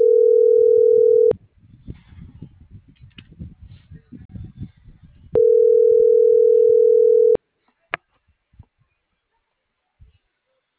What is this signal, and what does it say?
Ringback tone (North America), 440 + 480 Hz.